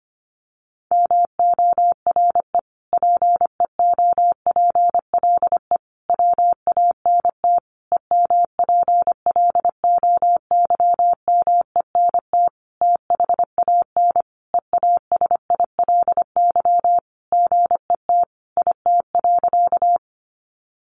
Morse code, heard as a keyed tone, about 25 wpm, 700 Hz. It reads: MORE PEOPLE WANT EMPLOYMENT THAN EASILY GET IT.
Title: Sentences from Top 1500 Words 25wpm